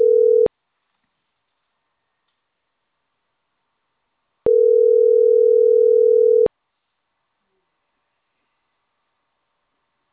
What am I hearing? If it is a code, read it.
Ringback tone (North America), 440 + 480 Hz.